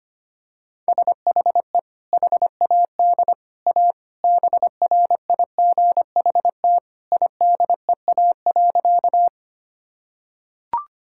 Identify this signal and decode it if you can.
Morse code, heard as a keyed tone, about 25 wpm, 700 Hz. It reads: SHE HAD A BRIGHT IDEA.
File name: practice / 25wpm / Sentences with 2-Word Phrases - 25wpm